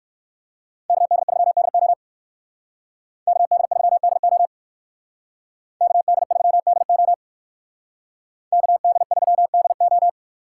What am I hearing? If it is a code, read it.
Morse code, heard as a keyed tone, about 58 wpm, 700 Hz. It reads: KD3DO KD3DO KD3DO KD3DO